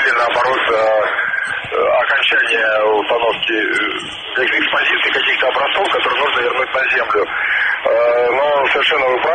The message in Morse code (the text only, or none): none